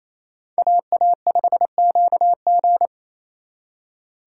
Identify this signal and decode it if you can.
Morse code, heard as a keyed tone, about 35 wpm, 700 Hz. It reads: AA5QG
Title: Speed Racing - USA Call Signs 28wpm